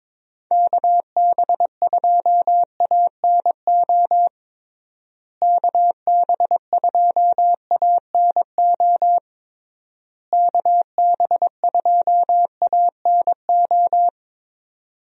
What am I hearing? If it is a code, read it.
Morse code, heard as a keyed tone, about 22 wpm, 700 Hz. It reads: KB2ANO KB2ANO KB2ANO
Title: USA Call Signs - 6 Characters - Repeated 3x - 22wpm